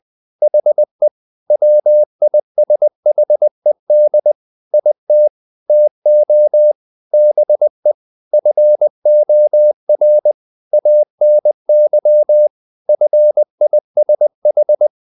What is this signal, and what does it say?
Morse code, heard as a keyed tone, about 20 wpm, 600 Hz. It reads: HE WISHED IT TO BE FOR ANY FISH
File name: the old man and the sea / the old man and the sea part 1 20wpm